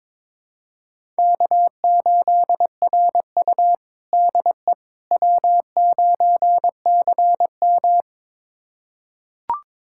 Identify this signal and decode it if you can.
Morse code, heard as a keyed tone, about 22 wpm, 700 Hz. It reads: K8RU DE W9CM
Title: USA Callsign DE Callsign - 4 Characters - 22wpm